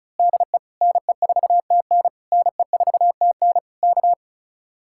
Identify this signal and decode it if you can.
Morse code, heard as a keyed tone, about 32 wpm, 700 Hz. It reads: DE NE4TN NE4TN K
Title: Speed-Racing Common QSO Phrases - 35wpm